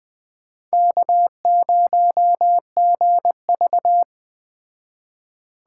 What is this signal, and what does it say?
Morse code, heard as a keyed tone, about 21 wpm, 700 Hz. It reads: K0GV
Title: Speed Racing - USA Call Signs 17wpm